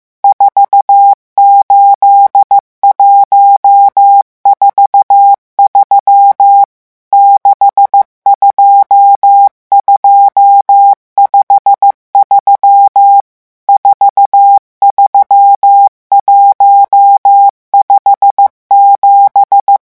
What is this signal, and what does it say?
Morse code, heard as a keyed tone, about 15 wpm, 800 Hz. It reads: 48143 62253 43157
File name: morse mp3 / revision11 chiffres vitesse 14